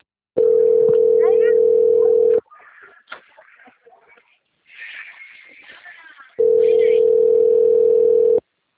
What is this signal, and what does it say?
Ringback tone (North America), 440 + 480 Hz, with speech.
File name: out- - - - - .23733